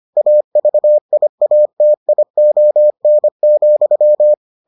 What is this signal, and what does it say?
Morse code, heard as a keyed tone, about 25 wpm, 600 Hz. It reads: AVIATION,